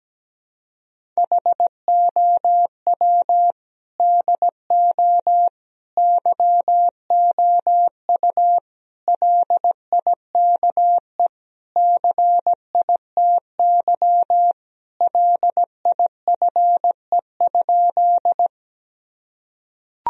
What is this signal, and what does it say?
Morse code, heard as a keyed tone, about 17 wpm, 700 Hz. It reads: HOW DO YOU LIKE CITY LIFE?
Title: Sentences from Top 1000 Words - Review Part 1 - 17wpm